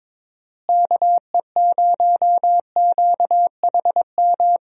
Morse code, heard as a keyed tone, about 22 wpm, 700 Hz. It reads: KE0QHM
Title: USA Call Signs - 6 Characters - Repeated 2x - 22wpm